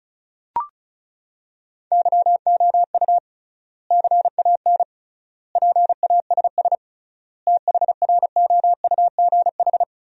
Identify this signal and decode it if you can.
Morse code, heard as a keyed tone, about 36 wpm, 700 Hz. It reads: YOU CAN PASS THROUGH
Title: 3x Word-Spacing - Sentences from Top 400 Words - 35wpm